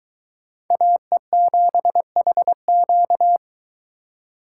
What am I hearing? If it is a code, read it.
Morse code, heard as a keyed tone, about 23 wpm, 700 Hz. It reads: AE7HQ